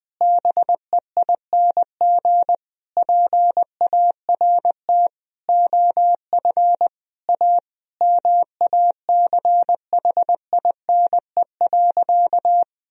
Morse code, heard as a keyed tone, about 20 wpm, 700 Hz. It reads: BEING PART OF A MACHINE.
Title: Sentences from Top 1000 Words - Review Part 3 - 20wpm